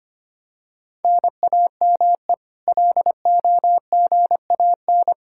Morse code, heard as a keyed tone, about 25 wpm, 700 Hz. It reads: NAME LOGAN
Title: Practice QSO - Exchange 1 - Beginning - 25wpm